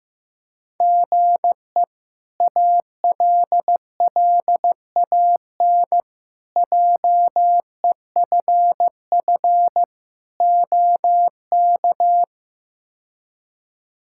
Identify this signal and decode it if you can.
Morse code, heard as a keyed tone, about 15 wpm, 700 Hz. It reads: GE ALLAN JEFF OK
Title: K1USN SST QSO Practice - 15wpm